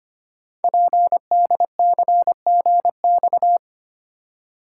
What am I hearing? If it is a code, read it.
Morse code, heard as a keyed tone, about 25 wpm, 700 Hz. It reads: PDCGX
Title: Code Groups - 5 Letters - 25wpm